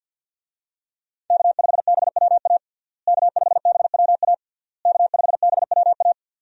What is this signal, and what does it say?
Morse code, heard as a keyed tone, about 51 wpm, 700 Hz. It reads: K5BWA K5BWA K5BWA
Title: USA Call Signs - 5 Characters - Repeated 3x - 50wpm